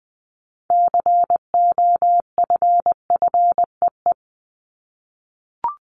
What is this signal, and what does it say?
Morse code, heard as a keyed tone, about 20 wpm, 700 Hz. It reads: COFFEE
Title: Top 1751-2000 Words 20wpm